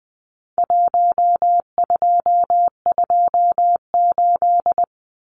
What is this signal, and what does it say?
Morse code, heard as a keyed tone, about 20 wpm, 700 Hz. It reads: 1228
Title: Numbers - Contest Style - 4-Digit 20wpm